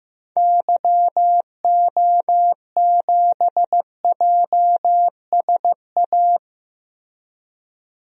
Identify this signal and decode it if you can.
Morse code, heard as a keyed tone, about 15 wpm, 700 Hz. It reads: YO7JSA
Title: DXCC-100 Call Signs - 6 Characters - 15wpm